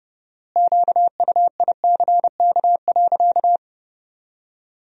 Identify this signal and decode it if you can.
Morse code, heard as a keyed tone, about 30 wpm, 700 Hz. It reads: QUICK.
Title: 4x Word-Spacing - Sentences from Top 500 Words - 30wpm